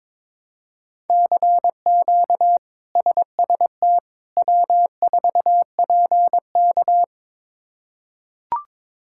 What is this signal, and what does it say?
Morse code, heard as a keyed tone, about 22 wpm, 700 Hz. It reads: CQ SST W4PK
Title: K1USN SST QSO Practice - 22wpm